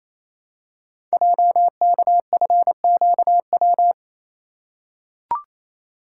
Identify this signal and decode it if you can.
Morse code, heard as a keyed tone, about 27 wpm, 700 Hz. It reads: JKFQW
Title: Code Groups - 5 Letters - 28wpm